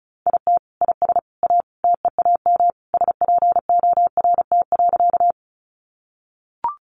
Morse code, heard as a keyed tone, about 34 wpm, 700 Hz. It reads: IT IS A TEAM SPORT.
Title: Sentences from Top 1500 Words 35wpm